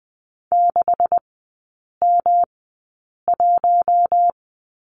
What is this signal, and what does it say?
Morse code, heard as a keyed tone, about 20 wpm, spaced as for 8 wpm, 690 Hz. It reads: 6M1
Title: 212 Lesson 34e - Number 8 - Three Characters 20wpm